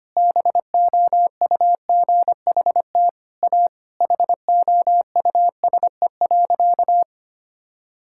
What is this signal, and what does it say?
Morse code, heard as a keyed tone, about 25 wpm, 700 Hz. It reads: BOUGHT A HOUSE.
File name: practice / 25wpm / Sentences with 4-Word Phrases - 25wpm